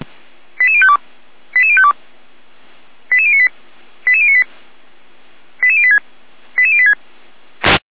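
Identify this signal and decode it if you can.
Five-tone selective call (ZVEI), heard as quick runs of five tones, tones of about 70 ms. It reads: 88052 88052 88098 88098 88086 88086
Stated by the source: Received by SDR.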